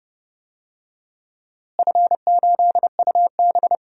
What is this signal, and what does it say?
Morse code, heard as a keyed tone, about 30 wpm, 700 Hz. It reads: F8UB
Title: DXCC-100 Call Signs - 4 Characters - Repeated 2x - 30wpm